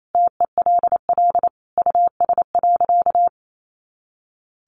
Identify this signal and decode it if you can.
Morse code, heard as a keyed tone, about 28 wpm, 700 Hz. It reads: TELL US.
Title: Sentences from Top 400 Words 28wpm